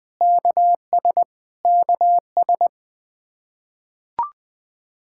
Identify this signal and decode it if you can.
Morse code, heard as a keyed tone, about 20 wpm, 700 Hz. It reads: KS KS
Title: Practice QSO - Exchange 1 - Beginning - 20wpm